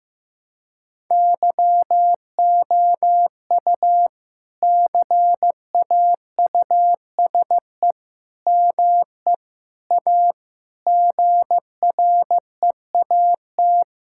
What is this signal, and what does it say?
Morse code, heard as a keyed tone, about 16 wpm, 700 Hz. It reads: YOU CAUSE ME A GREAT
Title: Sentences from Top 1600 Words 15wpm